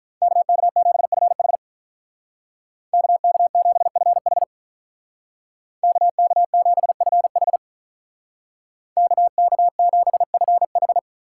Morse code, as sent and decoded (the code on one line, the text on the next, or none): -.- -.- --... ..-. .... / -.- -.- --... ..-. .... / -.- -.- --... ..-. .... / -.- -.- --... ..-. ....
KK7FH KK7FH KK7FH KK7FH